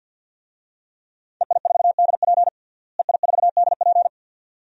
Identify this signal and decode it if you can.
Morse code, heard as a keyed tone, about 51 wpm, 700 Hz. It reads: EI4DP EI4DP
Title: DXCC-100 Call Signs - 5 Characters - Repeated 2x - 50wpm